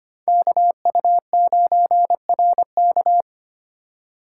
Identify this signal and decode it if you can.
Morse code, heard as a keyed tone, about 25 wpm, 700 Hz. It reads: KU9RK
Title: K1USN SST QSO Practice - 25wpm